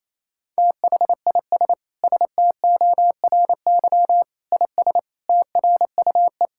Morse code, heard as a keyed tone, about 28 wpm, 700 Hz. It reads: THIS STORY IS TRUE
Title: Sentences from Top 1600 Words 28wpm